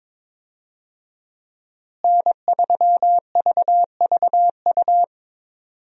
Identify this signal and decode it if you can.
Morse code, heard as a keyed tone, about 21 wpm, 700 Hz. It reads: N3VVU
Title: USA Call Signs - 5 Characters - Repeated 3x - 22wpm